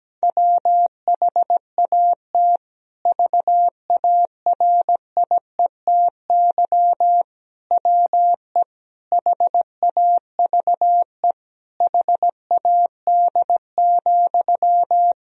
Morse code, heard as a keyed tone, about 17 wpm, 700 Hz. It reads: WHAT VARIETY WE HAVE HAD,
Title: Sentences from Top 1000 Words - Review Part 4 - 17wpm